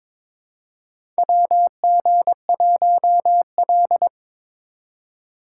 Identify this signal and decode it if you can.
Morse code, heard as a keyed tone, about 22 wpm, 700 Hz. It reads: WG1L
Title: USA Call Signs - 2x Character Spacing - 22wpm